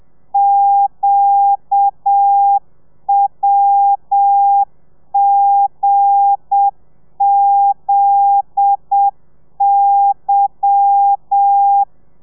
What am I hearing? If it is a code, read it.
Morse code, heard as a keyed tone, about 7 wpm, 800 Hz. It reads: QWGZY
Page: Lezione 1: audio a 30 CPM